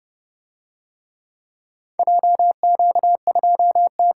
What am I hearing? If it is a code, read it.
Morse code, heard as a keyed tone, about 28 wpm, 700 Hz. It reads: JQ2T